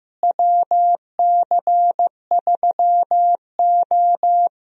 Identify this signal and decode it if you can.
Morse code, heard as a keyed tone, about 15 wpm, 700 Hz. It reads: WC3O
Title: USA Call Signs - 4 Characters - Repeated 2x - 15wpm